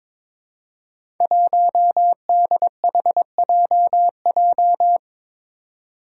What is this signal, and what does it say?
Morse code, heard as a keyed tone, about 22 wpm, 700 Hz. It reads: 1DHJJ
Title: Five Challenging Letter-Number Combinations 22wpm